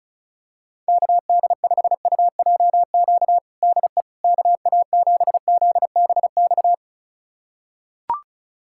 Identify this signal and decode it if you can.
Morse code, heard as a keyed tone, about 36 wpm, 700 Hz. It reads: KD5UJQ DE KA7ZBX
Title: USA Callsign DE Callsign - 6 Characters - 35wpm